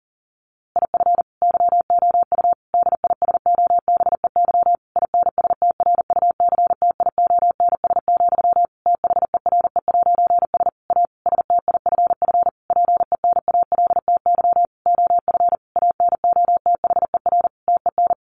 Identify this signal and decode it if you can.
Morse code, heard as a keyed tone, about 40 wpm, 700 Hz. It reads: IF YOU DISOBEY INSTRUCTIONS, THERE'S A STIFF PENALTY OF ANOTHER TEN
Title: A Sound Of Thunder 40wpm